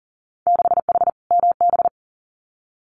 Morse code, heard as a keyed tone, about 40 wpm, 700 Hz. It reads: 6H MB
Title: Winter Field Day Exchanges Encore 40wpm